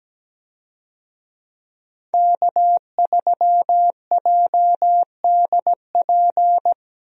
Morse code, heard as a keyed tone, about 17 wpm, 700 Hz. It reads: K3JDP
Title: USA Call Signs - 5 Characters - Repeated 2x - 17wpm